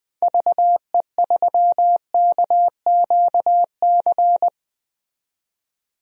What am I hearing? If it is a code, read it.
Morse code, heard as a keyed tone, about 20 wpm, 700 Hz. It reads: VE3KQC
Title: DXCC-100 Call Signs - 6 Characters - 20wpm